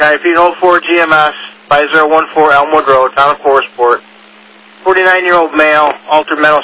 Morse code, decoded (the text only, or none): none